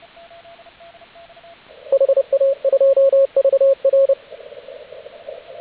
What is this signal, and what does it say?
Morse code, heard as a keyed tone, about 30 wpm, 540 Hz, with radio noise. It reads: HA2VR